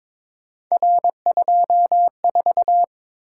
Morse code, heard as a keyed tone, about 22 wpm, 700 Hz. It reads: R24
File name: practice / 22wpm / Three Letter-Number Combinations - 22wpm